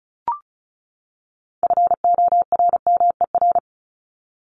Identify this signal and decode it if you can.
Morse code, heard as a keyed tone, about 34 wpm, 700 Hz. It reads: FORMER
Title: LY Suffix - Top 100 Words with Base Words 35wpm